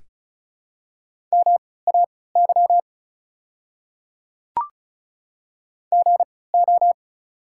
Morse code, heard as a keyed tone, about 36 wpm, spaced as for 20 wpm, 700 Hz. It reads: MAY GO
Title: 3x Character Spacing - Top 100 Words - 35wpm